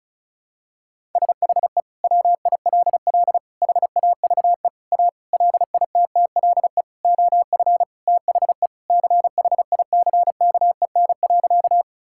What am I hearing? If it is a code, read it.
Morse code, heard as a keyed tone, about 36 wpm, 700 Hz. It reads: SHE WILL HAVE A LITTLE OF THE CHICKEN.
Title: Sentences from Top 1000 Words - Review Part 5 - 35wpm